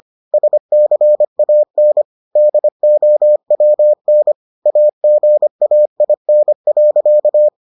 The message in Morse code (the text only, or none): SCAN DOWN AGAIN.